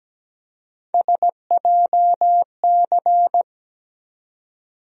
Morse code, heard as a keyed tone, about 17 wpm, 700 Hz. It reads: SJC
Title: USA Call Sign Postfixes - 3 Letters 17wpm